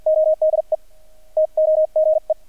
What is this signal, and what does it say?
Morse code, heard as a keyed tone, about 48 wpm, spaced as for 41 wpm, 620 Hz. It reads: ODE TOGE
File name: archived / CRSnet